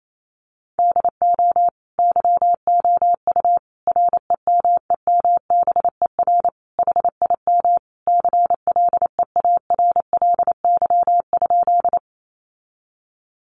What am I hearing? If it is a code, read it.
Morse code, heard as a keyed tone, about 28 wpm, 700 Hz. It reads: DO YOU REMEMBER HIM CLEARLY?
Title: Sentences from Top 1300 Words 28wpm